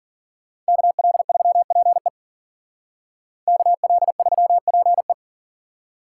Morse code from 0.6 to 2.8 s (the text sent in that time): KL3PE